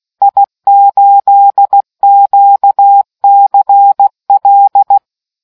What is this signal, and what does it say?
Morse code, heard as a keyed tone, about 16 wpm, 800 Hz. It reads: I8QCL